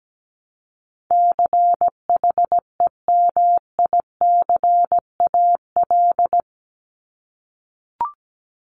Morse code, heard as a keyed tone, about 17 wpm, 700 Hz. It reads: CHEMICAL
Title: Top 1001-1250 Words 17wpm